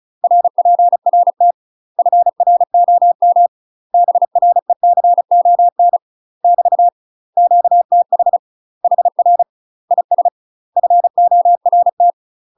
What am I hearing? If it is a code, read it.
Morse code, heard as a keyed tone, about 36 wpm, 700 Hz. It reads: RPRT FROM BRECON <BT> QTH HR IS FORT
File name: UK MASTER QSOs TRAINING / 35WPM / UK MASTER QSOs 0007